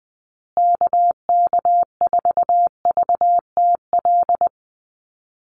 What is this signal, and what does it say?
Morse code, heard as a keyed tone, about 20 wpm, 700 Hz. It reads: KK4VTL
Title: USA Call Signs Repeated 2x 20wpm